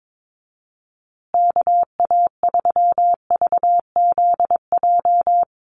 Morse code, heard as a keyed tone, about 22 wpm, 700 Hz. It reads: KA3VZJ
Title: USA Call Signs Repeated 3x 22wpm